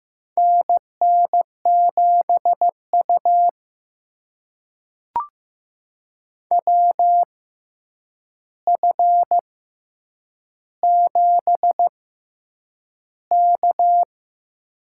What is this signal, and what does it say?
Morse code, heard as a keyed tone, about 15 wpm, spaced as for 8 wpm, 700 Hz. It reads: NN7U WF7K
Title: USA Call Signs - 4 Characters - 6x Character Spacing - 15wpm